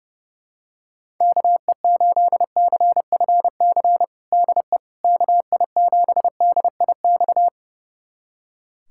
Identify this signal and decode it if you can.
Morse code, heard as a keyed tone, about 30 wpm, 700 Hz. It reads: KE8CFC DE KI7DIX
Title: USA Callsign DE Callsign - 6 Characters - 30wpm